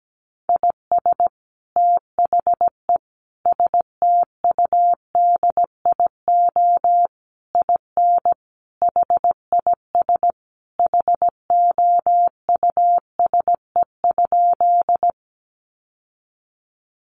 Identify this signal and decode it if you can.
Morse code, heard as a keyed tone, about 17 wpm, 700 Hz. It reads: IS THE STUDIO IN HIS HOUSE?
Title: Sentences from Top 1200 Words 17wpm